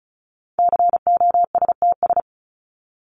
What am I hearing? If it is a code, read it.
Morse code, heard as a keyed tone, about 34 wpm, 700 Hz. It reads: COSTS
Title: ES:S Suffix - Top 100 Words with Base Words 35wpm